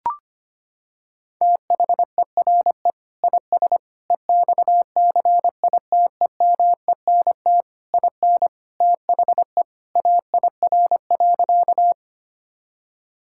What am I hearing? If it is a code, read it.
Morse code, heard as a keyed tone, about 25 wpm, 700 Hz. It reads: THERE IS EXCITEMENT IN THE AIR.
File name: practice / 25wpm / Sentences with 3-Word Phrases - 25wpm